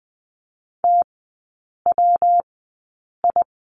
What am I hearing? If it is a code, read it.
Morse code, heard as a keyed tone, about 20 wpm, spaced as for 8 wpm, 700 Hz. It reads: TWI
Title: 114 Lesson 19e - Letter w - Three Characters 20wpm